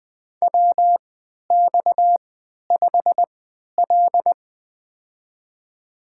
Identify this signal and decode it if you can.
Morse code, heard as a keyed tone, about 20 wpm, spaced as for 11 wpm, 700 Hz. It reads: WX5L
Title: USA Call Signs - 4 Characters - 3x Character Spacing - 20wpm